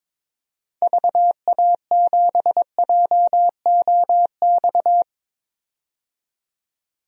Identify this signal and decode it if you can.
Morse code, heard as a keyed tone, about 22 wpm, 700 Hz. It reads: VA7JOX
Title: DXCC-100 Call Signs - 6 Characters - 22wpm